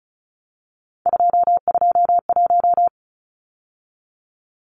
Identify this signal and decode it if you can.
Morse code, heard as a keyed tone, about 34 wpm, 700 Hz. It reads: 221
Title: Numbers - 3-Digit 35wpm